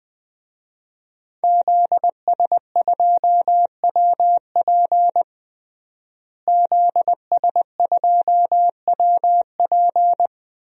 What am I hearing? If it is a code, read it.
Morse code, heard as a keyed tone, about 20 wpm, 700 Hz. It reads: ZS2WP ZS2WP
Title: DXCC-100 Call Signs - 5 Characters - Repeated 3x - 20wpm